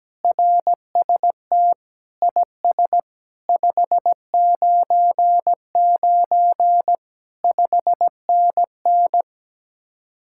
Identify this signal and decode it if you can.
Morse code, heard as a keyed tone, about 17 wpm, 700 Hz. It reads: RST IS 599 5NN
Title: Speed-Racing Common QSO Phrases - 17wpm